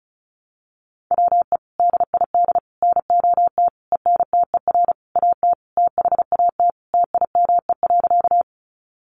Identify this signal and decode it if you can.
Morse code, heard as a keyed tone, about 34 wpm, 700 Hz. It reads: WE DID NOT ENTER AT THAT TIME.
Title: Sentences from Top 1300 Words 35wpm